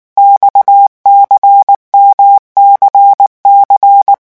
Morse code, heard as a keyed tone, about 19 wpm, 800 Hz. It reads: XCMCC
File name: revision1-vitesse 18 mots